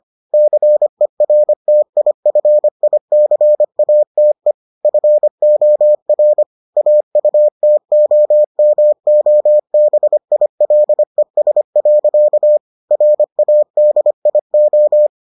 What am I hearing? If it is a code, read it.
Morse code, heard as a keyed tone, about 25 wpm, 600 Hz. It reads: CERTIFICATE FOR AUTOMOBILES. RADIO